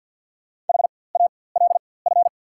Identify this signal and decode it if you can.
Morse code, heard as a keyed tone, about 49 wpm, spaced as for 23 wpm, 690 Hz. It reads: HALF
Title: 4x Character Spacing - Top 500 Words - 50wpm